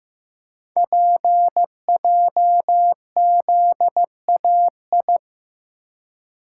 Morse code, heard as a keyed tone, about 15 wpm, 700 Hz. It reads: PJZAI
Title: Code Groups - 5 Letters - 15wpm